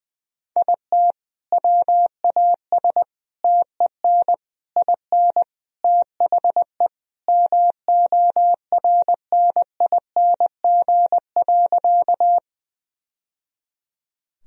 Morse code, heard as a keyed tone, about 20 wpm, 700 Hz. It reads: IT WAS TEN IN THE MORNING.
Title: Sentences from Top 1000 Words - Review Part 2 - 20wpm